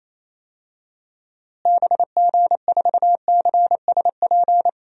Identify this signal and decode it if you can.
Morse code, heard as a keyed tone, about 27 wpm, 700 Hz. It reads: BG4CSP
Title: DXCC-100 Call Signs - 6 Characters - Repeated 3x - 28wpm